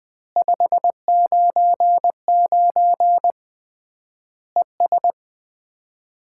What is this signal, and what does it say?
Morse code, heard as a keyed tone, about 21 wpm, 700 Hz. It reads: 599 ES
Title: Common QSO Phrases - 3x Word-Spacing - 20wpm